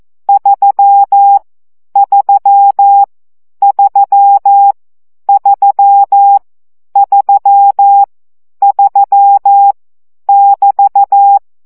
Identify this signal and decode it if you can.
Morse code, heard as a keyed tone, about 14 wpm, 800 Hz. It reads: 3 3 3 3 3 3 <BT>
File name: CW gyakorlas 3 12 baud